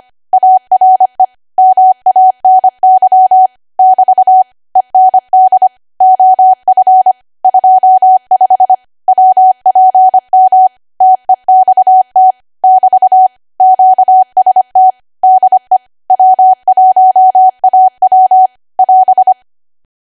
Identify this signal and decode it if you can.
Morse code, heard as a keyed tone, about 25 wpm, 750 Hz. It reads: ARE MANY <BT> END OF 25 WPM TEXT <BT> QST DE W1AW <AS>
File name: Archive / 25 WPM / 25WPM